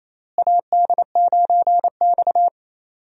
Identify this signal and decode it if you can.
Morse code, heard as a keyed tone, about 27 wpm, 700 Hz. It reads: AD9X
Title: USA Call Signs - 4 Characters - Repeated 2x - 28wpm